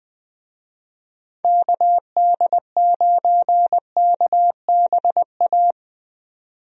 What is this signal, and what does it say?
Morse code, heard as a keyed tone, about 20 wpm, 700 Hz. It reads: KD9KBA
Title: USA Call Signs - 6 Characters - 20wpm